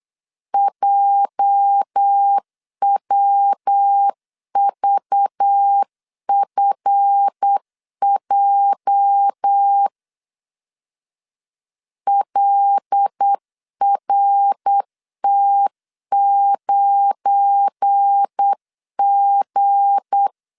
Morse code, heard as a keyed tone, about 8 wpm, 800 Hz. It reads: JWVFJ LRT9G
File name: LEZIONI / 9 28 8wpm 3512